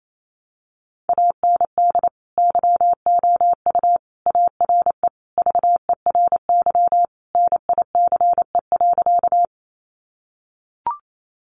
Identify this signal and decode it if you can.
Morse code, heard as a keyed tone, about 28 wpm, 700 Hz. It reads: AND YOU ARE VERY NICE.
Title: Sentences from Top 1100 Words 28wpm